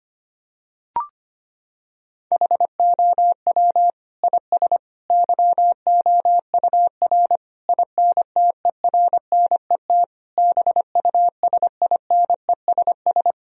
Morse code, heard as a keyed tone, about 25 wpm, 700 Hz. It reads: HOW IS YOUR INTERNET BUSINESS
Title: Sentences from Top 1000 Words - Review Part 4 - 25wpm